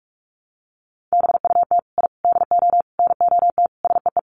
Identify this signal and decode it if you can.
Morse code, heard as a keyed tone, about 44 wpm, 700 Hz. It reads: BUT I DO NOT SEE
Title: Sets of 5 Words Encore 45wpm